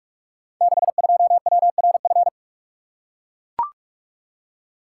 Morse code, heard as a keyed tone, about 44 wpm, 690 Hz. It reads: 62WRF
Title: Code Groups - 5 Letter:Numbers - 45wpm